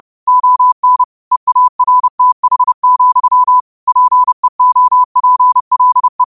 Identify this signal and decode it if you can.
Morse code, heard as a keyed tone, about 30 wpm, 1000 Hz. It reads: ON EARTH, PEOPLE
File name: qotdfiles / 30wpm